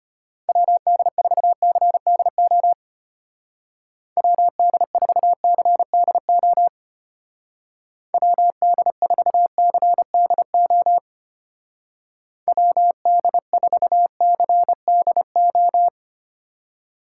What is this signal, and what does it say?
Morse code, heard as a keyed tone, about 31 wpm, 700 Hz. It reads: WD4CDO WD4CDO WD4CDO WD4CDO
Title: Speed Racing - USA 2x3 Call Signs 25wpm